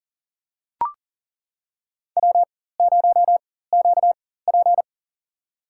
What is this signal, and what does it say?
Morse code, heard as a keyed tone, about 40 wpm, spaced as for 19 wpm, 700 Hz. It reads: W0QP
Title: USA Call Signs - 4 Characters - 4x Character Spacing - 40wpm